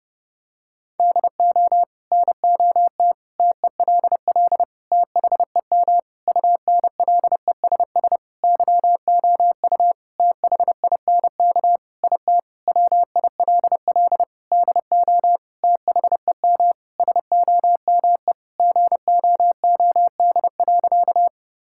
Morse code, heard as a keyed tone, about 30 wpm, 700 Hz. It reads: DO NOT TELL THEM UNLESS YOU THINK IT WILL DO THEM SOME GOOD.
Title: Sentences from Top 1000 Words - Review Part 6 - 30wpm